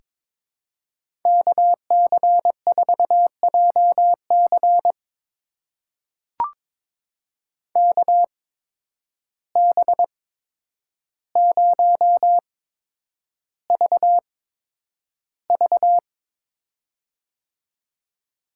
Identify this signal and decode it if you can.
Morse code, heard as a keyed tone, about 22 wpm, spaced as for 9 wpm, 700 Hz. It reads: KC4JC KB0VV
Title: USA Call Signs - 5 Characters - 8x Character Spacing - 22wpm